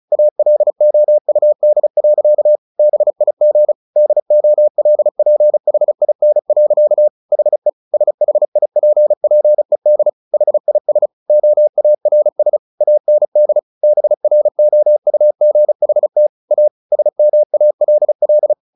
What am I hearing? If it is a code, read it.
Morse code, heard as a keyed tone, about 34 wpm, 600 Hz. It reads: ALOUD. BIG DOLPHIN. HE SHIPPED HIS OARS AND BROUGHT A SMALL
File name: the old man and the sea / the old man and the sea 1 35wpm